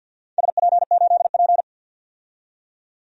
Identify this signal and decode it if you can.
Morse code, heard as a keyed tone, about 49 wpm, 700 Hz. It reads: SP8P
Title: DXCC-100 Call Signs - 4 Characters - 50wpm